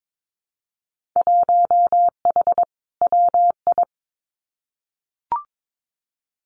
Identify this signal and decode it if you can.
Morse code, heard as a keyed tone, about 22 wpm, 700 Hz. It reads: 1H WI
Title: Winter Field Day Exchanges 22wpm